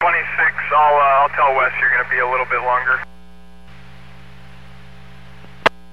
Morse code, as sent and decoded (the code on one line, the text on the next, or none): none